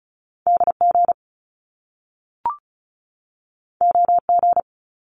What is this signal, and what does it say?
Morse code, heard as a keyed tone, about 36 wpm, 700 Hz. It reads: DG OG
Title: Top Two Letter Combinations in Words 35wpm